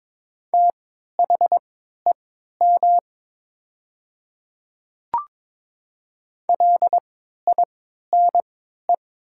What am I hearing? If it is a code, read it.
Morse code, heard as a keyed tone, about 22 wpm, spaced as for 12 wpm, 700 Hz. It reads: THEM LINE
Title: 3x Character Spacing - Top 200 Words - 22wpm